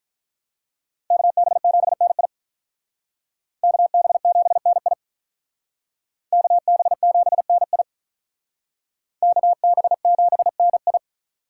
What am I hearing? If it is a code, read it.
Morse code, heard as a keyed tone, about 43 wpm, 700 Hz. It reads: KB7NI KB7NI KB7NI KB7NI
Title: Speed Racing - USA 2x2 Call Signs 35wpm